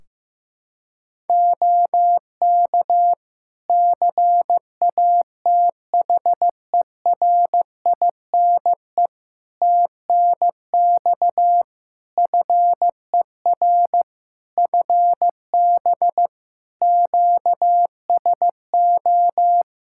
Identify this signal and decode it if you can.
Morse code, heard as a keyed tone, about 15 wpm, 700 Hz. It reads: OK CATHERINE TNX FER FB QSO
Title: Practice QSO - Exchange 4 - Ending - 15wpm